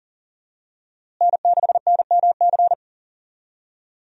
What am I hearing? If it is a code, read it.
Morse code, heard as a keyed tone, about 40 wpm, 700 Hz. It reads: N6NMC
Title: USA Call Signs - 5 Characters - Repeated 2x - 40wpm